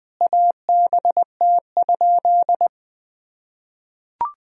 Morse code, heard as a keyed tone, about 20 wpm, 700 Hz. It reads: ABT?
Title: Sentences from Top 100 Words and Abbreviations 20wpm